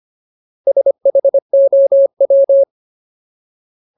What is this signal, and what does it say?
Morse code, heard as a keyed tone, about 25 wpm, 550 Hz. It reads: SHOW
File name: Ultimate Word List / 25WPM Ultimate